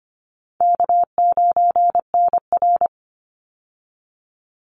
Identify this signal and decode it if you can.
Morse code, heard as a keyed tone, about 25 wpm, 700 Hz. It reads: K9NR
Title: Easy Call Signs Repeated 2x - 25wpm